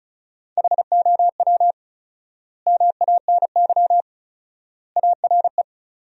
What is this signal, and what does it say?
Morse code, heard as a keyed tone, about 34 wpm, 700 Hz. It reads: HOW MANY ARE
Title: 4x Word-Spacing - Sentences from Top 300 Words - 35wpm